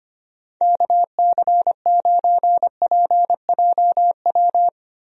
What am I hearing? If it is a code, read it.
Morse code, heard as a keyed tone, about 25 wpm, 700 Hz. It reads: KC9PJW
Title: USA Call Signs - 6 Characters - Repeated 3x - 25wpm